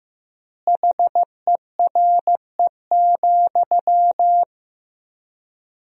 Morse code, heard as a keyed tone, about 15 wpm, 700 Hz. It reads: HERE,
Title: 4x Word-Spacing - Sentences from Top 500 Words - 15wpm